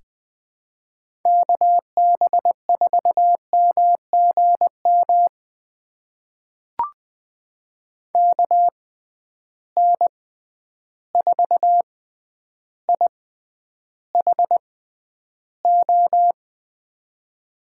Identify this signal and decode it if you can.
Morse code, heard as a keyed tone, about 20 wpm, spaced as for 10 wpm, 700 Hz. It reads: KB4MGM KN4IHO
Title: USA Call Signs - 6 Characters - 6x Character Spacing - 20wpm